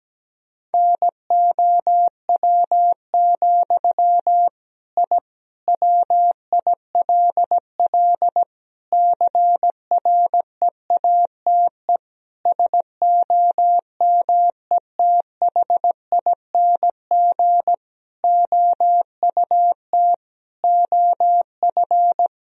Morse code, heard as a keyed tone, about 17 wpm, 700 Hz. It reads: NOW, I WILL CREATE SOMETHING OUT OF
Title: Sentences from Top 1000 Words - Review Part 3 - 17wpm